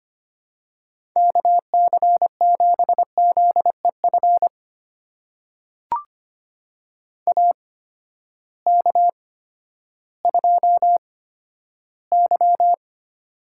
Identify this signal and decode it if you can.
Morse code, heard as a keyed tone, about 25 wpm, spaced as for 12 wpm, 700 Hz. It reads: KC7ZEF AK2Y
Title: USA Call Signs - 8x Character Spacing - 25wpm